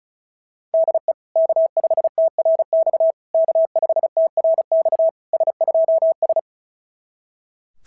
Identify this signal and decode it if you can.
Morse code, heard as a keyed tone, about 36 wpm, 690 Hz. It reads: DE K5TRX K5TRX S2S
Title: SOTA QSO Practice 35wpm